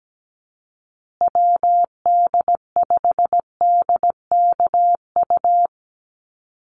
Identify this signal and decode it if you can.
Morse code, heard as a keyed tone, about 17 wpm, 700 Hz. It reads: WD5DKU
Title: USA Call Signs Repeated 3x 17wpm